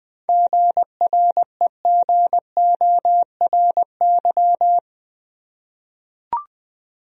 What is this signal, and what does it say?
Morse code, heard as a keyed tone, about 20 wpm, 700 Hz. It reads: GREGORY
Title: USA Common Names - 20wpm